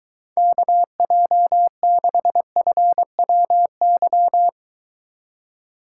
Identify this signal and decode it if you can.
Morse code, heard as a keyed tone, about 23 wpm, 700 Hz. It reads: KJ6FWY